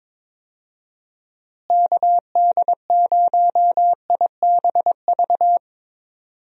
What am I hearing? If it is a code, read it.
Morse code, heard as a keyed tone, about 22 wpm, 700 Hz. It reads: KD0IBV
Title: USA Call Signs - 6 Characters - Repeated 2x - 22wpm